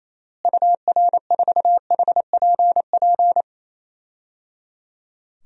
Morse code, heard as a keyed tone, about 28 wpm, 700 Hz. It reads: UR4HPP
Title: DXCC-100 Call Signs - 6 Characters - Repeated 3x - 28wpm